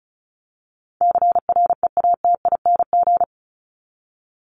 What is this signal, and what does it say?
Morse code, heard as a keyed tone, about 34 wpm, 700 Hz. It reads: CREATING
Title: ING Suffix - Top 100 Words with Base Words 35wpm